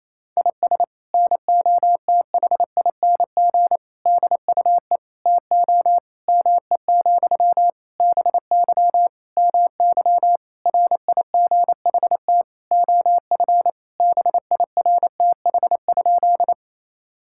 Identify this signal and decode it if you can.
Morse code, heard as a keyed tone, about 28 wpm, 700 Hz. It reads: IS NOTHING DUE TO ME, BY MY RIGHT OF BIRTH?
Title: Sentences from Top 1000 Words - Review Part 6 - 28wpm